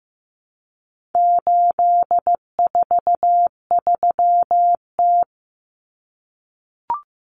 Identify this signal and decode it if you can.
Morse code, heard as a keyed tone, about 15 wpm, 700 Hz. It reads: 843T
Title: Common Cut Numbers - 4-Digit 15wpm